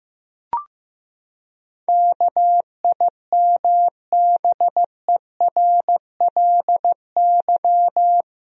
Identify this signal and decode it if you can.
Morse code, heard as a keyed tone, about 15 wpm, 700 Hz. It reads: KIMBERLY
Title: USA Common Names - 15wpm